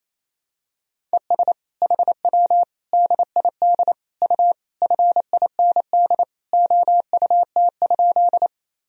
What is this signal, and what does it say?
Morse code, heard as a keyed tone, about 28 wpm, 700 Hz. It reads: ES HW DID U FIND OUT?
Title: Sentences from Top 200 Words and Abbreviations 28wpm